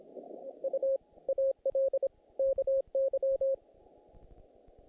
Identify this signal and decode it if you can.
Morse code, heard as a keyed tone, about 26 wpm, 540 Hz. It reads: U AL KY
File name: contests / naqp-cw-k5tr-op-k5pi- -2100